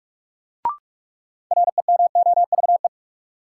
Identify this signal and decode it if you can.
Morse code, heard as a keyed tone, about 46 wpm, 700 Hz. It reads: REMOVE
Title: Sentences from Top 800 Words - New Word Drill - 45wpm